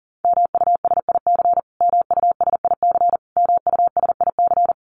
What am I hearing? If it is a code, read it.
Morse code, heard as a keyed tone, about 40 wpm, 700 Hz. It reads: MUSIC MUSIC MUSIC
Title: Top 500 Words Repeated 40wpm